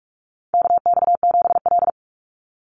Morse code, heard as a keyed tone, about 44 wpm, 700 Hz. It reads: KX7L
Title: NA SOTA Chaser Callsigns 45wpm